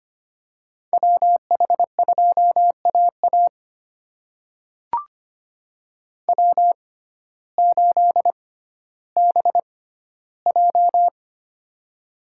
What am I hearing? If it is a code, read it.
Morse code, heard as a keyed tone, about 25 wpm, spaced as for 14 wpm, 700 Hz. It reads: WH2AA W8BJ
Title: USA Call Signs - 6x Character Spacing - 25wpm